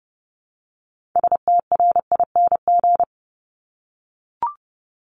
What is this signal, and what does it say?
Morse code, heard as a keyed tone, about 30 wpm, 700 Hz. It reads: STRING
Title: Multi-letter Phonemes 30wpm